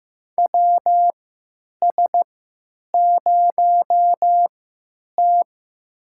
Morse code, heard as a keyed tone, about 15 wpm, spaced as for 9 wpm, 700 Hz. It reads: WS0T
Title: USA Call Signs - 4 Characters - 3x Character Spacing - 15wpm